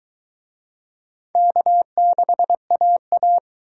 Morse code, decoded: K6AA

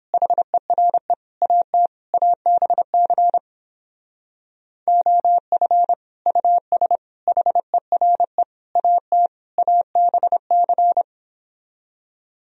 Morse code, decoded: HERE AT ABC OF US HERE AT ABC

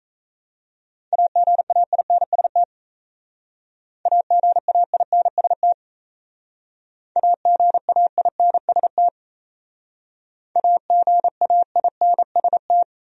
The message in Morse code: AGAINST AGAINST AGAINST AGAINST

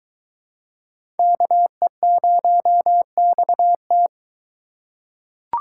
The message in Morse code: KE0XT

